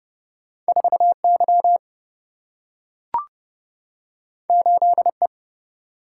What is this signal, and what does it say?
Morse code, heard as a keyed tone, about 30 wpm, 700 Hz. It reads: 4Y 8E
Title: Single-Letter Single-Number 30wpm